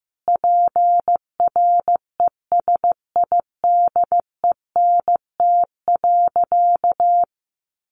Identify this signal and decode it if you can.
Morse code, heard as a keyed tone, about 15 wpm, 700 Hz. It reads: PRESIDENT.
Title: Sentences from Top 1500 Words 15wpm